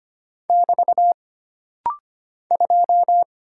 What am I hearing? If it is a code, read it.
Morse code, heard as a keyed tone, about 25 wpm, 700 Hz. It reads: <BT> 2
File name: Single Character (Letter:Number:Punctuation:Prosign) - Rapid-Fire - 25wpm